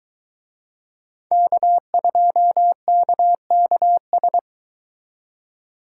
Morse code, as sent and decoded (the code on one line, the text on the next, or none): -.- ..--- -.- -.- ...
K2KKS